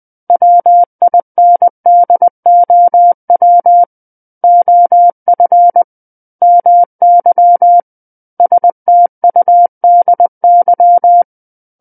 Morse code, heard as a keyed tone, about 20 wpm, 700 Hz. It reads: WINDOW OF MY STUDY
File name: 20 WPM